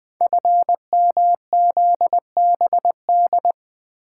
Morse code, heard as a keyed tone, about 20 wpm, 700 Hz. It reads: FMZBD
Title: Code Groups - 5 Letters - 20wpm